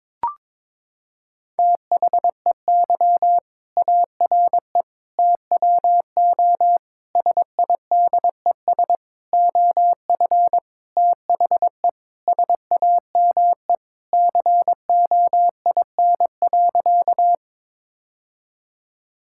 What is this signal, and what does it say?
Morse code, heard as a keyed tone, about 22 wpm, 700 Hz. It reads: THEY ARE TWO SIDES OF THE SAME COIN.
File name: Sentences with 7-Word Phrases - 22wpm